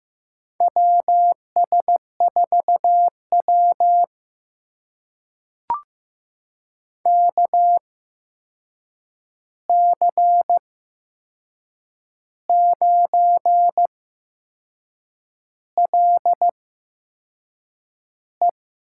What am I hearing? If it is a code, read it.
Morse code, heard as a keyed tone, about 15 wpm, spaced as for 6 wpm, 700 Hz. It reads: WS4W KC9LE
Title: USA Call Signs - 8x Character Spacing - 15wpm